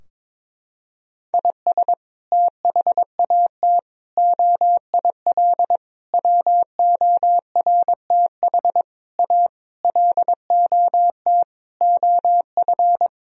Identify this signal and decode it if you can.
Morse code, heard as a keyed tone, about 22 wpm, 700 Hz. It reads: IS THAT OIL WORTH A LOT OF
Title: Sentences from Top 1000 Words - Review Part 4 - 22wpm